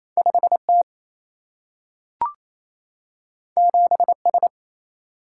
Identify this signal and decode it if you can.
Morse code, heard as a keyed tone, about 28 wpm, 700 Hz. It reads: 5T 7S
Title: Single-Letter Single-Number 28wpm